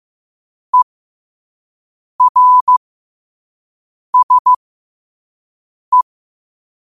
Morse code, heard as a keyed tone, about 15 wpm, spaced as for 5 wpm, 1000 Hz. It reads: ERSE
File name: qotdfiles / 5wpm